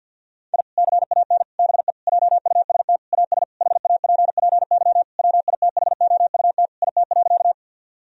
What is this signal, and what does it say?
Morse code, heard as a keyed tone, about 50 wpm, 700 Hz. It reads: I CAN BE JUST AS HAPPY WITHOUT IT.
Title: Sentences from Top 1000 Words - Review Part 3 - 50wpm